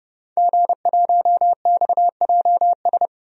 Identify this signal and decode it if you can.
Morse code, heard as a keyed tone, about 32 wpm, 700 Hz. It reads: G1XJS
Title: DXCC-100 Call Signs - 5 Characters - 30wpm